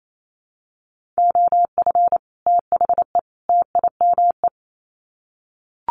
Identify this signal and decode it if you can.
Morse code, heard as a keyed tone, about 28 wpm, 700 Hz. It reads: OF THE TIME
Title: Sets of 3 Words 28wpm